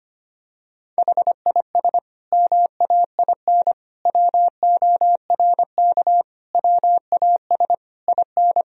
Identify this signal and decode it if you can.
Morse code, heard as a keyed tone, about 25 wpm, 700 Hz. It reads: HIS MAIN WORK WAS IN
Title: Sentences from Top 1000 Words - Review Part 2 - 25wpm